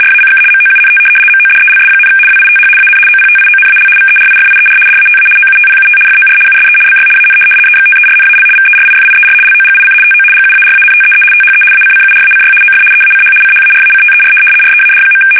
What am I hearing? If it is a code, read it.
RTTY, about 75 baud. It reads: RACK MOUNT MODELS, WITH AN OPTIONAL VERIFIER DATA OUTPUT FOR USE BY AN EXTERNAL VERIFIER LOGIC TO GUARANTEE THE ACCURACY OF EACH CHARACTER READ.
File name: fsk850-75B / teletype-corp